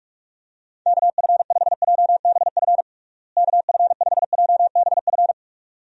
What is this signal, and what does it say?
Morse code, heard as a keyed tone, about 46 wpm, 700 Hz. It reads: KF5JBF KF5JBF
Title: USA Call Signs - 6 Characters - Repeated 2x - 45wpm